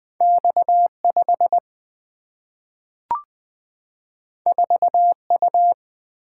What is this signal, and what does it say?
Morse code, heard as a keyed tone, about 20 wpm, 700 Hz. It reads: X5 4U
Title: Single-Letter Single-Number 20wpm